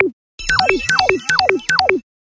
There is no code.